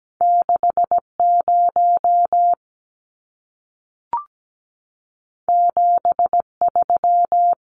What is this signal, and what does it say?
Morse code, heard as a keyed tone, about 17 wpm, 700 Hz. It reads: 60 73
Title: Numbers - 2-Digit 17wpm